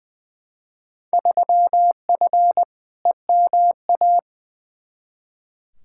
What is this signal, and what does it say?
Morse code, heard as a keyed tone, about 20 wpm, 700 Hz. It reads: 3F EMA
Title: Field Day Exchanges Encore - 20wpm